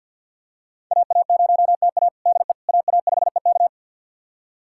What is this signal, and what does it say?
Morse code, heard as a keyed tone, about 47 wpm, 700 Hz. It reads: AA0TA DE AA5EK
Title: USA Callsign DE Callsign - 5 Characters - 50wpm